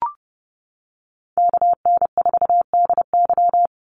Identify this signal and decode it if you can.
Morse code, heard as a keyed tone, about 30 wpm, 700 Hz. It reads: KN4DY
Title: USA Call Signs 30wpm